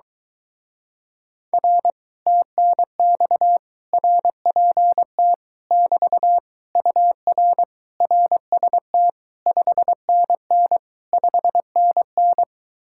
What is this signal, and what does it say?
Morse code, heard as a keyed tone, about 24 wpm, 700 Hz. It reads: R TNX RPT <BT> UR RST 5NN 5NN